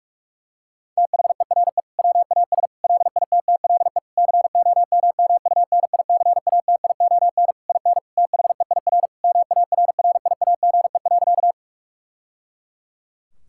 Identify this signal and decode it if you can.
Morse code, heard as a keyed tone, about 46 wpm, 700 Hz. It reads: THERE WAS LITTLE COMMUNICATION IN THEIR MARRIAGE.
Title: Sentences from Top 1000 Words - Review Part 6 - 45wpm